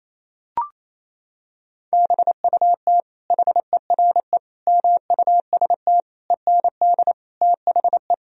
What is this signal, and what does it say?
Morse code, heard as a keyed tone, about 27 wpm, 700 Hz. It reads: BUT HERE MUST END THE
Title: Sentences from Top 1000 Words - Review Part 1 - 28wpm